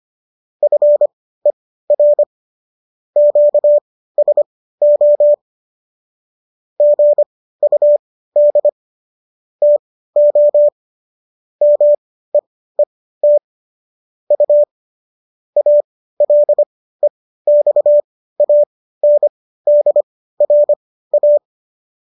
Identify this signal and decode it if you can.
Morse code, heard as a keyed tone, about 25 wpm, spaced as for 15 wpm, 590 Hz. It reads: FER QSO GUD TO MEET U ALEXANDRA